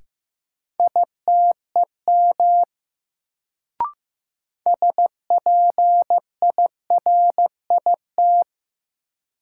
Morse code, heard as a keyed tone, about 15 wpm, 700 Hz. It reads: ITEM SPIRIT
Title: Sentences from Top 1400 Words - New Word Drill Alt - 15wpm